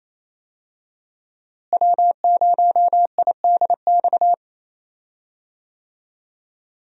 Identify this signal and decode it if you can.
Morse code, heard as a keyed tone, about 28 wpm, 700 Hz. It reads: W0IDX
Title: USA Call Signs - 5 Characters - Repeated 3x - 28wpm